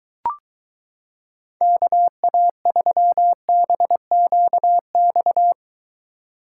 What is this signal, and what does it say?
Morse code, heard as a keyed tone, about 23 wpm, 700 Hz. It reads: KA3BQX